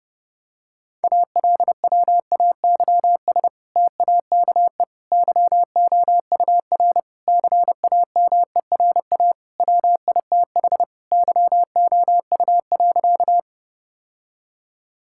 Morse code, heard as a keyed tone, about 30 wpm, 700 Hz. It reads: ALWAYS TAKE YOUR CAMERA WITH YOU.
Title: Sentences from Top 1000 Words - Review Part 5 - 30wpm